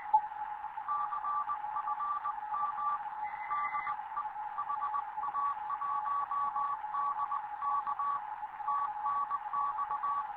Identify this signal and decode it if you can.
Morse code, heard as a keyed tone, about 20 wpm, 1140 Hz, with radio noise. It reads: CFM DE HA1DK TNX
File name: Morze-HA2MN / hangp / 2 11 GYAKORLAT